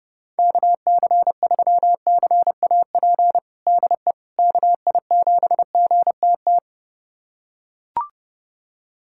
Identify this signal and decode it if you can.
Morse code, heard as a keyed tone, about 30 wpm, 700 Hz. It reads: KC3CAP DE KI7GTT
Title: USA Callsign DE Callsign 30wpm